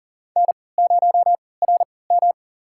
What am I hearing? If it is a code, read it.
Morse code, heard as a keyed tone, about 40 wpm, spaced as for 23 wpm, 700 Hz. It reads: N0RM
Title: USA Call Signs - 4 Characters - 3x Character Spacing - 40wpm